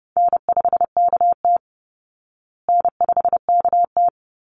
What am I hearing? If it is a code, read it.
Morse code, heard as a keyed tone, about 30 wpm, 700 Hz. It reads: N5KT N5KT
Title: Call Signs Repeated 3x - 30wpm